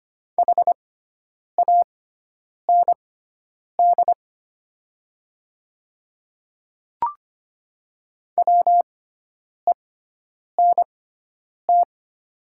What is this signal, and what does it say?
Morse code, heard as a keyed tone, about 25 wpm, spaced as for 9 wpm, 700 Hz. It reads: HAND WENT
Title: 6x Character Spacing - Top 200 Words - 25wpm